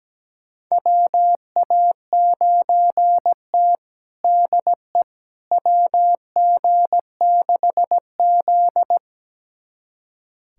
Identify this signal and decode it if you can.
Morse code, heard as a keyed tone, about 17 wpm, 700 Hz. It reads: WA9T DE WG6Z
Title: USA Callsign DE Callsign - 4 Characters - 17wpm